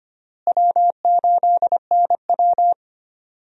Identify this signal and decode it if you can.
Morse code, heard as a keyed tone, about 25 wpm, 700 Hz. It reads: W8NW
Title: USA Call Signs - 4 Characters - Repeated 3x - 25wpm